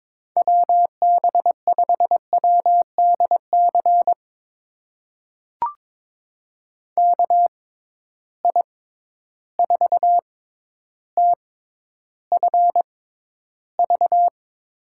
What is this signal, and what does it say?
Morse code, heard as a keyed tone, about 22 wpm, spaced as for 11 wpm, 700 Hz. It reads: WB5WDC KI4TFV
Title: USA Call Signs - 6 Characters - 6x Character Spacing - 22wpm